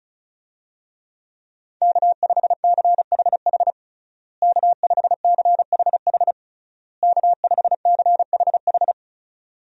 Morse code, heard as a keyed tone, about 36 wpm, 700 Hz. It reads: K5CHH K5CHH K5CHH
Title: USA Call Signs - 5 Characters - Repeated 3x - 35wpm